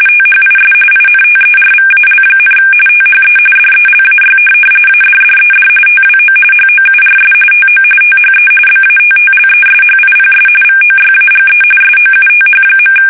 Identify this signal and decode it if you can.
RTTY, about 45.45 baud. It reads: THIS INTERVAL, THE DELAY ADDED TWO HOURS TO THE BATTLESHIPS VOYAGE SOUTH.
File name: fsk850 / USS-Massachusetts